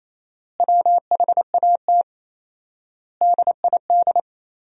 Morse code, heard as a keyed tone, about 28 wpm, 700 Hz. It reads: WHAT DID
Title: 4x Word-Spacing - Sentences from Top 500 Words - 28wpm